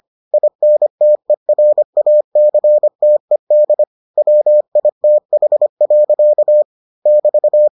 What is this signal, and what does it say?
Morse code, heard as a keyed tone, about 25 wpm, 600 Hz. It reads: INTERACTED WITH. <BT>